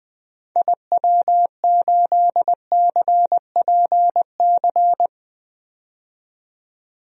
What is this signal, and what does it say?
Morse code, heard as a keyed tone, about 20 wpm, 700 Hz. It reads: IW8CPC
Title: DXCC-100 Call Signs - 6 Characters - 20wpm